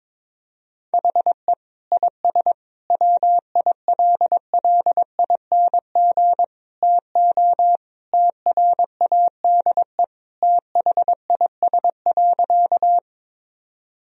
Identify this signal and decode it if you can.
Morse code, heard as a keyed tone, about 22 wpm, 700 Hz. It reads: HE IS WILLING TO TRADE THIS.
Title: Sentences from Top 1000 Words - Review Part 4 - 22wpm